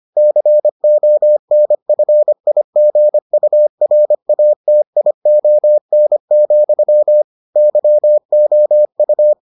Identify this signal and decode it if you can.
Morse code, heard as a keyed tone, about 25 wpm, 600 Hz. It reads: CONFIGURATION, YOU